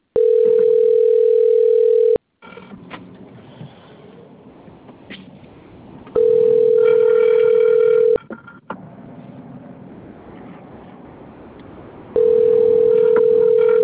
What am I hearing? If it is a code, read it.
Ringback tone (North America), 440 + 480 Hz.